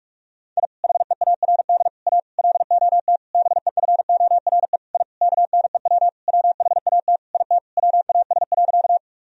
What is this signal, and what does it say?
Morse code, heard as a keyed tone, about 45 wpm, 700 Hz. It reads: I HEARD A LOT BEFORE I KNEW WHAT IT WAS.
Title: Sentences from Top 1000 Words - Review Part 3 - 45wpm